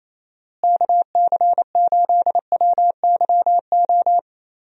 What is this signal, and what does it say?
Morse code, heard as a keyed tone, about 28 wpm, 700 Hz. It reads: KC8WYO
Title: USA Call Signs - 6 Characters - Repeated 2x - 28wpm